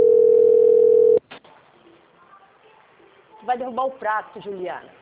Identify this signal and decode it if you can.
Ringback tone (North America), 440 + 480 Hz, with speech.
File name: out- - - - - .32046